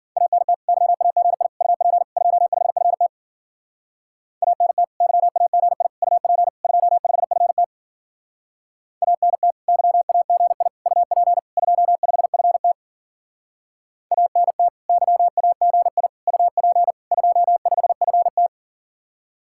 Morse code, heard as a keyed tone, about 50 wpm, 700 Hz. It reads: ANT YAGI UP 25FT ANT YAGI UP 25FT ANT YAGI UP 25FT ANT YAGI UP 25FT